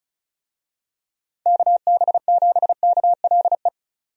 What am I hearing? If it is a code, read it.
Morse code, heard as a keyed tone, about 34 wpm, 700 Hz. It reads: KB7KLE
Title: USA Call Signs - 6 Characters - 35wpm